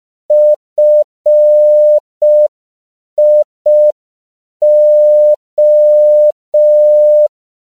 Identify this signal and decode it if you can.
Morse code, heard as a keyed tone, about 5 wpm, 600 Hz. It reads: FIO
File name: CW / 5PPM